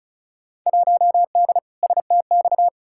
Morse code, heard as a keyed tone, about 36 wpm, 700 Hz. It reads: 1D STX
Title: Field Day Exchanges - 35wpm